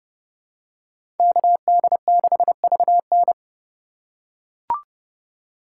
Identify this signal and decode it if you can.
Morse code, heard as a keyed tone, about 30 wpm, 700 Hz. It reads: KD6VN